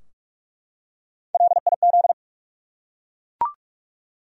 Morse code, heard as a keyed tone, about 41 wpm, 700 Hz. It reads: LIZ
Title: USA Common Names - Max 3 Letters - 45wpm